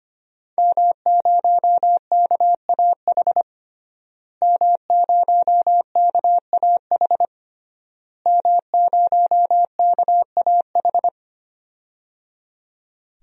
Morse code, heard as a keyed tone, about 25 wpm, 700 Hz. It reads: M0KAH M0KAH M0KAH
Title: DXCC-100 Call Signs - 5 Characters - Repeated 3x - 25wpm